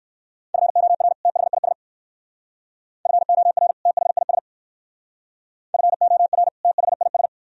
Morse code, heard as a keyed tone, about 57 wpm, 700 Hz. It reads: FOR THIS FOR THIS FOR THIS